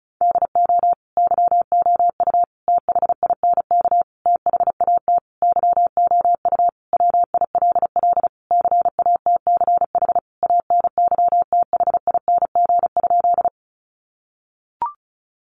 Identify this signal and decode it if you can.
Morse code, heard as a keyed tone, about 34 wpm, 700 Hz. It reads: DO YOU THINK THAT YOU WILL CATCH ANYTHING?
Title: Sentences from Top 1500 Words 35wpm